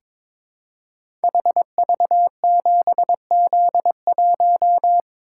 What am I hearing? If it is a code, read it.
Morse code, heard as a keyed tone, about 22 wpm, 700 Hz. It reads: HV7Z1
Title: Code Groups - 5 Letter:Numbers - 22wpm